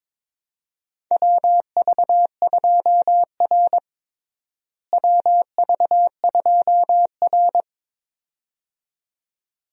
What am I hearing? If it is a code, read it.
Morse code, heard as a keyed tone, about 21 wpm, 700 Hz. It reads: WV2R WV2R
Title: USA Call Signs - 4 Characters - Repeated 2x - 22wpm